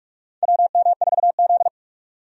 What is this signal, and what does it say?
Morse code, heard as a keyed tone, about 43 wpm, 700 Hz. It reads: WM4Z
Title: USA Call Signs - 4 Characters - 45wpm